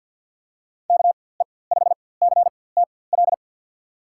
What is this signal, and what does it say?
Morse code, heard as a keyed tone, about 48 wpm, spaced as for 23 wpm, 700 Hz. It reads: KE5CTL
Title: USA Call Signs - 6 Characters - 4x Character Spacing - 50wpm